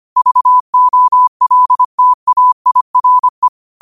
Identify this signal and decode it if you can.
Morse code, heard as a keyed tone, about 25 wpm, 1000 Hz. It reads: VOLTAIRE
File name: qotdfiles / 25wpm